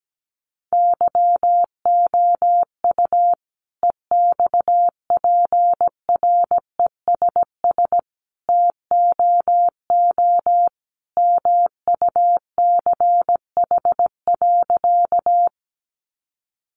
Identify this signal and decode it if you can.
Morse code, heard as a keyed tone, about 17 wpm, 700 Hz. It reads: YOU EXPRESS TOO MUCH.
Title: Sentences from Top 1100 Words 17wpm